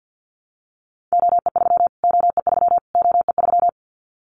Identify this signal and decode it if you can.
Morse code, heard as a keyed tone, about 50 wpm, 700 Hz. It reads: OE3 OE3 OE3
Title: Callsign Prefixes for Top 25 DXCC - Repeated 3x - 50wpm